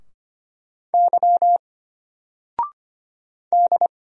Morse code, heard as a keyed tone, about 25 wpm, 700 Hz. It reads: Y D
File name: Single-Letter-Number-25wpm